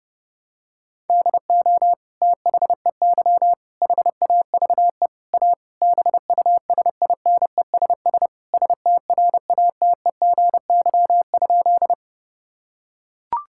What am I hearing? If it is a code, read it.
Morse code, heard as a keyed tone, about 30 wpm, 700 Hz. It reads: DO THEY HAVE A BUSINESS STRATEGY?
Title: Sentences from Top 1000 Words - Review Part 5 - 30wpm